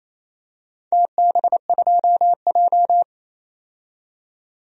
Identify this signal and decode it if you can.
Morse code, heard as a keyed tone, about 28 wpm, 700 Hz. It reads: TB2J
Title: DXCC-100 Call Signs - 4 Characters - 28wpm